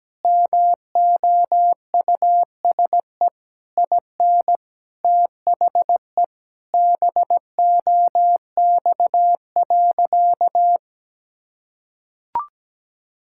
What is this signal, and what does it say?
Morse code, heard as a keyed tone, about 17 wpm, 700 Hz. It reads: MOUSE IN THE BOX.
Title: Sentences from Top 1000 Words - Review Part 5 - 17wpm